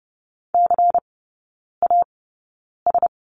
Morse code, heard as a keyed tone, about 30 wpm, spaced as for 9 wpm, 690 Hz. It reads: CAS
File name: Lesson 16e - Letter c - Three Characters 30wpm